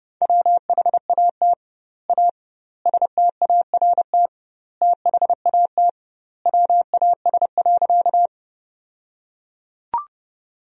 Morse code, heard as a keyed tone, about 32 wpm, 700 Hz. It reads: WHAT A START THAT WAS.
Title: 2x Word-Spacing - Sentences from Top 300 Words - 30wpm